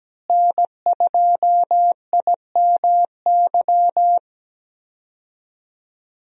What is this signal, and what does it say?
Morse code, heard as a keyed tone, about 17 wpm, 700 Hz. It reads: N2IMY
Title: USA Call Signs - 5 Characters - Repeated 2x - 17wpm